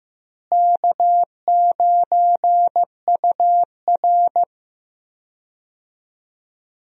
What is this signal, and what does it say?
Morse code, heard as a keyed tone, about 15 wpm, 700 Hz. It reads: K9UR
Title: USA Call Signs - 4 Characters - Repeated 2x - 15wpm